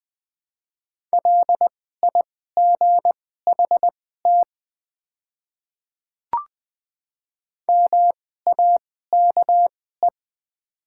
Morse code, heard as a keyed tone, about 20 wpm, spaced as for 14 wpm, 700 Hz. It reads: LIGHT MAKE
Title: 2x Character Spacing - Top 500 Words - 20wpm